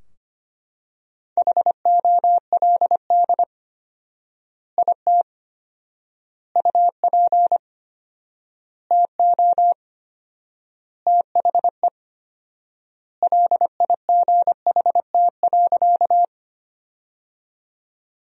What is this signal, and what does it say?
Morse code, heard as a keyed tone, about 25 wpm, 700 Hz. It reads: HOLD IT UP TO THE LIGHT.
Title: 4x Word-Spacing - Sentences from Top 400 Words - 25wpm